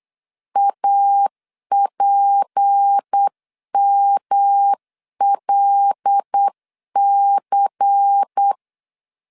Morse code, heard as a keyed tone, about 8 wpm, 800 Hz. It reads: APMLC
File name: CW / DET / 1 6 1500C S C 8wpm 5219